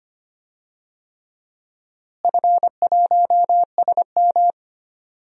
Morse code, heard as a keyed tone, about 25 wpm, 700 Hz. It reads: F1SM
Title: DXCC-100 Call Signs - 4 Characters - Repeated 3x - 25wpm